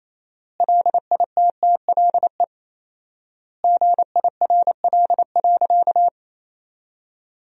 Morse code, heard as a keyed tone, about 28 wpm, 700 Hz. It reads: LITTLE GIRL.
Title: 4x Word-Spacing - Sentences from Top 400 Words - 28wpm